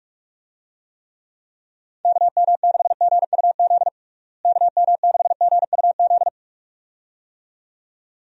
Morse code, heard as a keyed tone, about 46 wpm, 700 Hz. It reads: KM6GUZ KM6GUZ
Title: USA Call Signs - 6 Characters - Repeated 2x - 45wpm